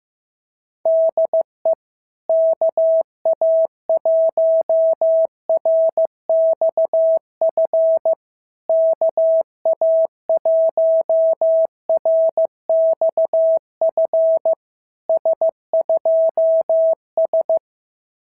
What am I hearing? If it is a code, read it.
Morse code, heard as a keyed tone, about 14 wpm, 650 Hz. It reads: DE KA1RXF KA1RXF S2S
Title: SOTA QSO Practice 15wpm